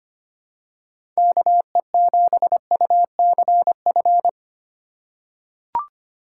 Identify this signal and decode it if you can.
Morse code, heard as a keyed tone, about 25 wpm, 700 Hz. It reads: KE7UCF
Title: USA Call Signs - 6 Characters - 6x Character Spacing - 25wpm